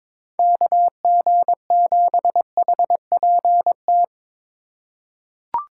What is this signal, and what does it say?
Morse code, heard as a keyed tone, about 22 wpm, 700 Hz. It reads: KG7HPT
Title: USA Call Signs - 6 Characters - 3x Character Spacing - 22wpm